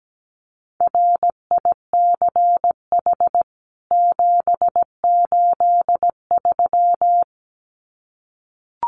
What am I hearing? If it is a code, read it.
Morse code, heard as a keyed tone, about 17 wpm, 700 Hz. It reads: RICH 783
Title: CWT Exchange 17wpm